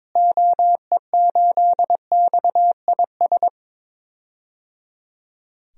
Morse code, heard as a keyed tone, about 22 wpm, 700 Hz. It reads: OE8XIS
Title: DXCC-100 Call Signs - 6 Characters - Repeated 3x - 22wpm